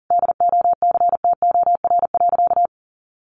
Morse code, heard as a keyed tone, about 40 wpm, 700 Hz. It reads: DOCTOR.
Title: Sentences from Top 1500 Words 40wpm